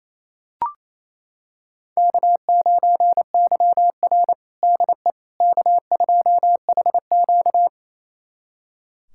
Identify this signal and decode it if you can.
Morse code, heard as a keyed tone, about 28 wpm, 700 Hz. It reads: K9YR DE K2HQ
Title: USA Callsign DE Callsign - 4 Characters - 28wpm